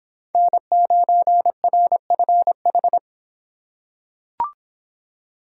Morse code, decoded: N9RFH

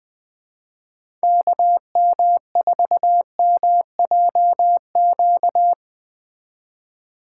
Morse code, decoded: KM4MJQ

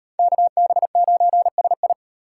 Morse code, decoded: KB9SI

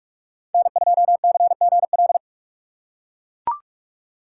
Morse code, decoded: N2CGL

